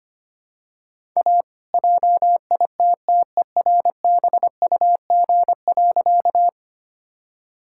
Morse code, heard as a keyed tone, about 25 wpm, 700 Hz. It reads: A JITTERBUG.
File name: practice / 25wpm / All-Age Appropriate Jokes - 25wpm